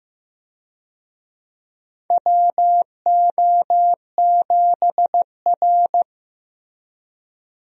Morse code, heard as a keyed tone, about 15 wpm, 700 Hz. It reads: WO7R
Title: USA Call Signs - 4 Characters - Repeated 3x - 15wpm